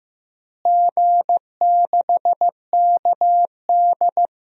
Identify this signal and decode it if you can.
Morse code, heard as a keyed tone, about 15 wpm, 700 Hz. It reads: G6KD
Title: DXCC-100 Call Signs - 4 Characters - Repeated 3x - 15wpm